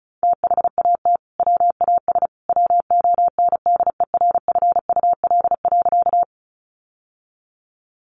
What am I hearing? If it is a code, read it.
Morse code, heard as a keyed tone, about 34 wpm, 700 Hz. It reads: THAT WAS WONDERFUL.
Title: Sentences from Top 1100 Words 35wpm